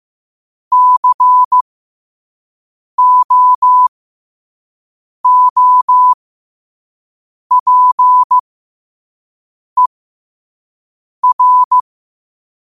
Morse code, heard as a keyed tone, about 15 wpm, spaced as for 5 wpm, 1000 Hz. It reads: COOPER
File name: qotdfiles / 5wpm